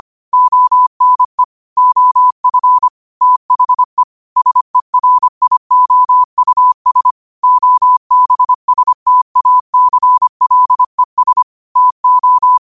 Morse code, heard as a keyed tone, about 25 wpm, 1000 Hz. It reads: ONE OF THE SERIOUS OBSTACLES TO
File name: qotdfiles / 25wpm